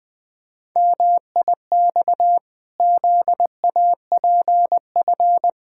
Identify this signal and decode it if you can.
Morse code, heard as a keyed tone, about 20 wpm, 700 Hz. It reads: MIX ZAPF
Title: Pangram Challenge v2 - 20wpm